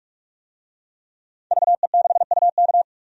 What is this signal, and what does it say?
Morse code, heard as a keyed tone, about 46 wpm, 700 Hz. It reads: VE6UK
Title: DXCC-100 Call Signs - 5 Characters - 45wpm